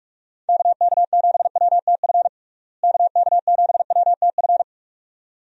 Morse code, heard as a keyed tone, about 46 wpm, 700 Hz. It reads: KK7WTF KK7WTF
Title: USA Call Signs - 6 Characters - Repeated 2x - 45wpm